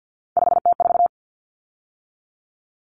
Morse code, heard as a keyed tone, about 50 wpm, 700 Hz. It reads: 5T4
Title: Common Cut Numbers - 3-Digit 50wpm